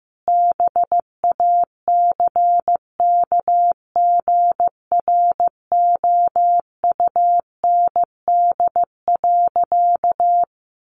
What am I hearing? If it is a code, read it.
Morse code, heard as a keyed tone, about 15 wpm, 700 Hz. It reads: BACKGROUND.
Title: Sentences from Top 1300 Words 15wpm